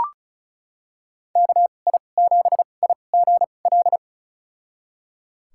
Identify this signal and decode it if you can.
Morse code, heard as a keyed tone, about 34 wpm, spaced as for 25 wpm, 700 Hz. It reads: KI7IGL
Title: USA Call Signs - 2x Character Spacing - 35wpm